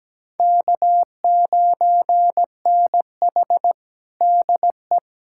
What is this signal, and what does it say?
Morse code, heard as a keyed tone, about 17 wpm, 700 Hz. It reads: K9NH DE
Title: USA Callsign DE Callsign - 4 Characters - 17wpm